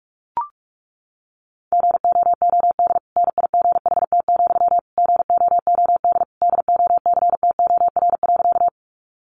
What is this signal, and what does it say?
Morse code, heard as a keyed tone, about 44 wpm, 700 Hz. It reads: GOOD NIGHT, GOOD DOCTOR.
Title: Sentences from Top 1500 Words 45wpm